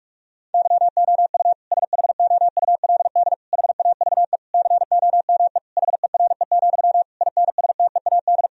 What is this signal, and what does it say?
Morse code, heard as a keyed tone, about 46 wpm, 700 Hz. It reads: YOU SHOULD HAVE COME HERE, INSTEAD
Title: Sentences from Top 1000 Words - Review Part 4 - 45wpm